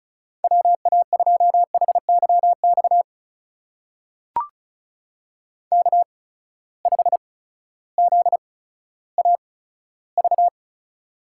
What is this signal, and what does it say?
Morse code, heard as a keyed tone, about 36 wpm, spaced as for 16 wpm, 700 Hz. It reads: WA2HYX K5ZAV
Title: USA Call Signs - 8x Character Spacing - 35wpm